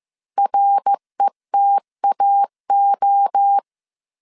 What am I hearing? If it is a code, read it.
Morse code, heard as a keyed tone, about 15 wpm, 800 Hz. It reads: RETAO
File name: CW / DET / 1 6 1500C S C 14wpm 2951